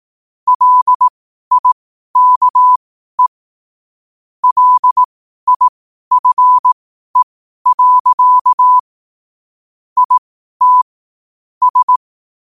Morse code, heard as a keyed tone, about 18 wpm, spaced as for 12 wpm, 1000 Hz. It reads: LIKE LIFE. IT S
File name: 12wpm